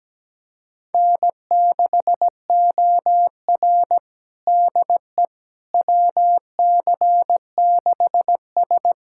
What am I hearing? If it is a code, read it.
Morse code, heard as a keyed tone, about 17 wpm, 700 Hz. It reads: N6OR DE WC6S
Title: USA Callsign DE Callsign - 4 Characters - 17wpm